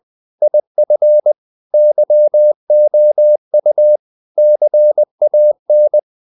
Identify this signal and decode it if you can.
Morse code, heard as a keyed tone, about 20 wpm, 600 Hz. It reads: IF YOU CAN
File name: CW / Practice / If20